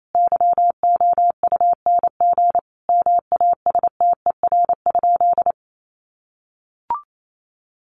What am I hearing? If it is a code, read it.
Morse code, heard as a keyed tone, about 28 wpm, 700 Hz. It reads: YOUNG MASTER?
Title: Sentences from Top 1300 Words 28wpm